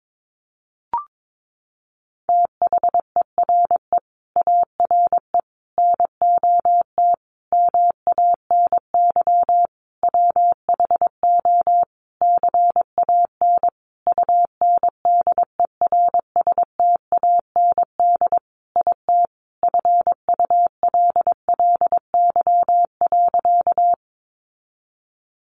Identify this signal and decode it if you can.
Morse code, heard as a keyed tone, about 22 wpm, 700 Hz. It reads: THERE ARE NOT MANY WHO CAN UNDERSTAND IT FULLY.
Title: Sentences from Top 1300 Words 22wpm